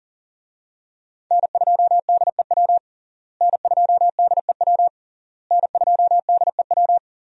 Morse code, heard as a keyed tone, about 40 wpm, 700 Hz. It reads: N2DEW N2DEW N2DEW
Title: USA Call Signs - 5 Characters - Repeated 3x - 40wpm